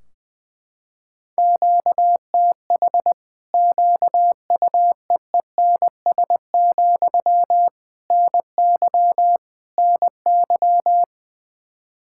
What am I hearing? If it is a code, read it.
Morse code, heard as a keyed tone, about 21 wpm, 700 Hz. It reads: QTH QUEENS, NY NY
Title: Practice QSO - Exchange 1 - Beginning - 20wpm